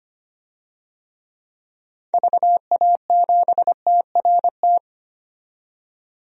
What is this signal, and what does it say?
Morse code, heard as a keyed tone, about 25 wpm, 700 Hz. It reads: VA7TRT
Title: DXCC-100 Call Signs - 6 Characters - 25wpm